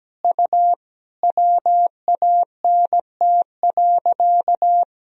Morse code, heard as a keyed tone, about 17 wpm, 700 Hz. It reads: U WANT.
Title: Sentences from Top 200 Words and Abbreviations 17wpm